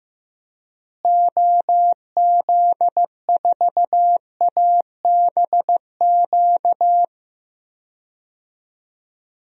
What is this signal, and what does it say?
Morse code, heard as a keyed tone, about 15 wpm, 700 Hz. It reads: OZ4ABQ
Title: DXCC-100 Call Signs - 6 Characters - Repeated 2x - 15wpm